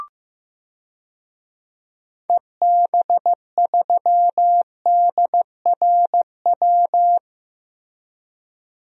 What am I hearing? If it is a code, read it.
Morse code, heard as a keyed tone, about 15 wpm, 700 Hz. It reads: EB3DRW
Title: DXCC-100 Call Signs - 6 Characters - 15wpm